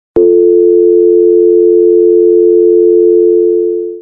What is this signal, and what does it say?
Dial tone (North America), 350 + 440 Hz.
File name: Phone Dial Tone - Sound Effect (HD)